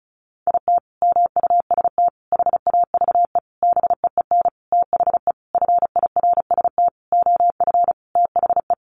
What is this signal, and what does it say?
Morse code, heard as a keyed tone, about 34 wpm, 700 Hz. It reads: IT MUST HAVE BEEN THE FIRST OF THE
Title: Sentences from Top 800 Words 35wpm